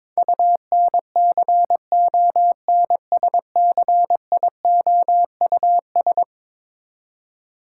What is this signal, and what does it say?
Morse code, heard as a keyed tone, about 22 wpm, 700 Hz. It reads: UNCONSCIOUS
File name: practice / 22wpm / Eleven Letter Words - 22wpm